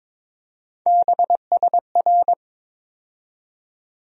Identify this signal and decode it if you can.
Morse code, heard as a keyed tone, about 21 wpm, 700 Hz. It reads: BSR
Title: USA Call Sign Postfixes - 3 Letters 22wpm